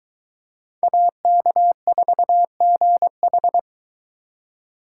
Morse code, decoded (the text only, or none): AK4GH